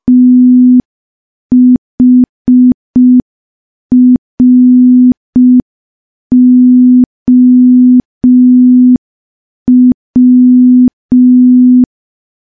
Morse code, heard as a keyed tone, about 5 wpm, 250 Hz. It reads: THROW
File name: morse / morse-training-05WPM